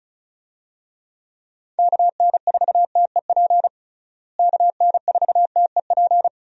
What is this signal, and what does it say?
Morse code, heard as a keyed tone, about 36 wpm, 700 Hz. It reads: KN4TEP KN4TEP
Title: USA Call Signs - 6 Characters - Repeated 3x - 35wpm